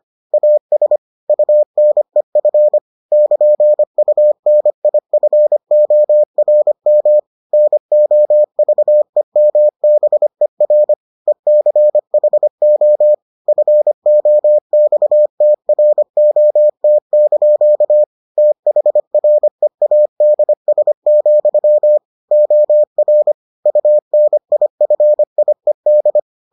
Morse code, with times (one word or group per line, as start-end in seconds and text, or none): 0.3-1.0 AS
1.3-2.8 UNEF
3.1-7.2 <KN>UNIFORM
7.5-10.9 NOVEMBER
11.3-13.1 ECHO
13.5-18.0 FOXTROT)
18.4-22.0 THREADS,
22.3-23.3 OR
23.7-26.2 UNIFIED